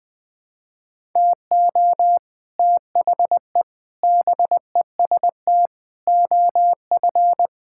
Morse code, heard as a keyed tone, about 20 wpm, 700 Hz. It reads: TO THE BEST OF
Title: Sentences with 8-Word Phrases - 20wpm